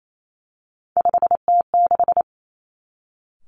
Morse code, heard as a keyed tone, about 28 wpm, 700 Hz. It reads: 5T6
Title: Three Challenging Letter-Number Combinations 28wpm